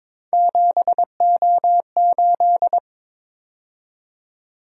Morse code, heard as a keyed tone, about 27 wpm, 700 Hz. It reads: 7O8
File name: Speed-Racing - Three Letter-Number Combinations - 22wpm